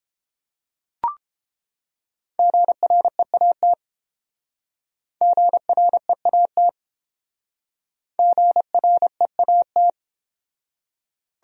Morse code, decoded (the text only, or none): GREAT GREAT GREAT